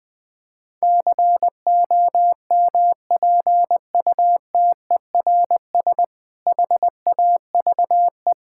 Morse code, heard as a keyed tone, about 21 wpm, 700 Hz. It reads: COMPUTERS HAVE
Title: Difficult Sentences with Hints - 20wpm